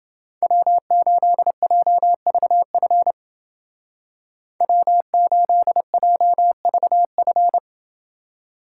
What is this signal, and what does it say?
Morse code, heard as a keyed tone, about 28 wpm, 700 Hz. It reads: W8JVF W8JVF